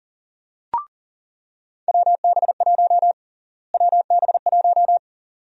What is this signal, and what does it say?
Morse code, heard as a keyed tone, about 40 wpm, 700 Hz. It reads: WB1 WB1
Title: USA Call Sign Prefixes - Repeated 2x - 40wpm